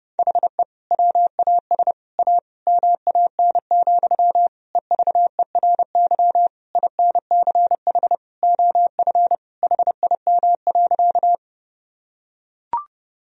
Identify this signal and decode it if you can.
Morse code, heard as a keyed tone, about 32 wpm, 700 Hz. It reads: HE WAS A MAN, EVERY INCH OF HIM.
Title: Sentences from Top 1000 Words - Review Part 3 - 30wpm